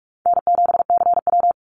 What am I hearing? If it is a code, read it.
Morse code, heard as a keyed tone, about 44 wpm, 700 Hz. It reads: N7CW
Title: NA SOTA Activator Callsigns 45wpm